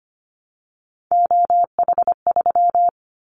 Morse code, heard as a keyed tone, about 25 wpm, 700 Hz. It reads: OH3
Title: Callsign Prefixes for Top 25 DXCC - Repeated 3x - 25wpm